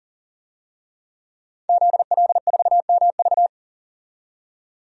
Morse code, heard as a keyed tone, about 40 wpm, 700 Hz. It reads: ZL4MV
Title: DXCC-100 Call Signs - 5 Characters - 40wpm